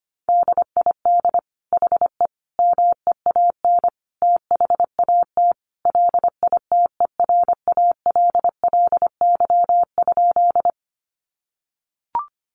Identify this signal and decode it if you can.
Morse code, heard as a keyed tone, about 25 wpm, 700 Hz. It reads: DID HE MEAN THAT LITERALLY?
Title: Sentences from Top 1400 Words 25wpm